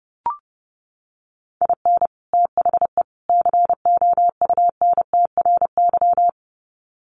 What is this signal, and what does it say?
Morse code, heard as a keyed tone, about 30 wpm, 700 Hz. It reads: IN THE COUNTRY
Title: Sets of 3 Words 30wpm